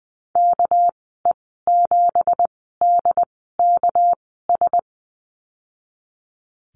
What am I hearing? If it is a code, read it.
Morse code, heard as a keyed tone, about 20 wpm, spaced as for 15 wpm, 700 Hz. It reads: KE7DKS
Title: 200 Lesson 32g - Letter k - Call Signs 20wpm